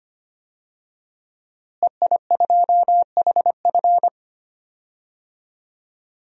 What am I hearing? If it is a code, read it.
Morse code, heard as a keyed tone, about 25 wpm, 700 Hz. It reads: EI2HF
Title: DXCC-100 Call Signs - 5 Characters - Repeated 2x - 25wpm